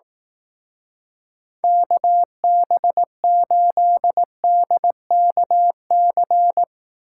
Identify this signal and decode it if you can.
Morse code, heard as a keyed tone, about 18 wpm, 700 Hz. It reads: KB8DKC
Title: Speed Racing - USA 2x3 Call Signs 15wpm